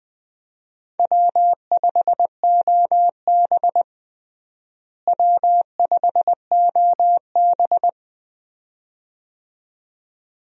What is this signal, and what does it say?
Morse code, heard as a keyed tone, about 20 wpm, 700 Hz. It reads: W5OB W5OB
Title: USA Call Signs - 4 Characters - Repeated 2x - 20wpm